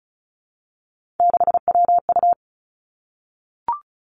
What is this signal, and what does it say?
Morse code, heard as a keyed tone, about 36 wpm, 700 Hz. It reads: 6WU
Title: Three Challenging Letter-Number Combinations 35wpm